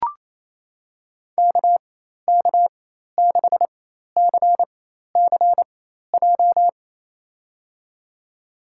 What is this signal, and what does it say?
Morse code, heard as a keyed tone, about 28 wpm, spaced as for 13 wpm, 700 Hz. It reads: KK6CCJ
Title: USA Call Signs - 4x Character Spacing - 28wpm